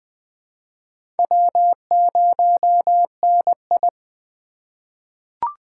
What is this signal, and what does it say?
Morse code, heard as a keyed tone, about 20 wpm, 700 Hz. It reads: W0NI
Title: USA Call Signs - 4 Characters - 4x Character Spacing - 20wpm